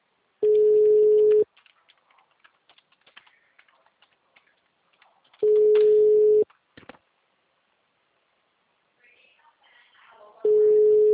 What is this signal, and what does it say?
Ringback tone (Europe, 425 Hz).